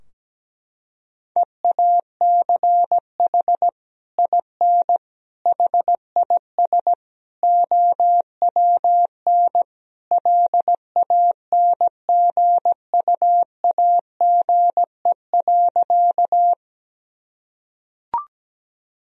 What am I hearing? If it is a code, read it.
Morse code, heard as a keyed tone, about 17 wpm, 700 Hz. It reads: EACH IN HIS OWN LANGUAGE.
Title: Sentences from Top 1000 Words - Review Part 3 - 17wpm